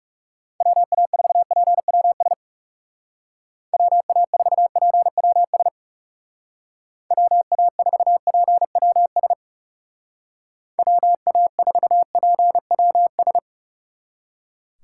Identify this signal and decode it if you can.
Morse code, heard as a keyed tone, about 36 wpm, 700 Hz. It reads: WA4PWS WA4PWS WA4PWS WA4PWS
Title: Speed Racing - USA 2x3 Call Signs 30wpm